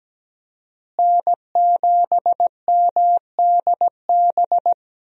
Morse code, heard as a keyed tone, about 17 wpm, 700 Hz. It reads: N7MDB
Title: USA Call Signs - 5 Characters - Repeated 3x - 17wpm